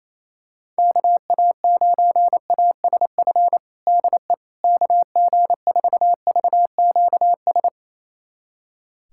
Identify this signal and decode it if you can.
Morse code, heard as a keyed tone, about 28 wpm, 700 Hz. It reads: KA9ASF DE KG4VQS
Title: USA Callsign DE Callsign 28wpm